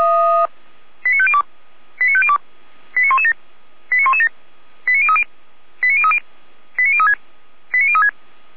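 Five-tone selective call (ZVEI), heard as quick runs of five tones, tones of about 70 ms. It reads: 89502 89502 89117 89117 89030 89030 89036 89036
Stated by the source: ZVEI five-tone call 89092 received by SDR.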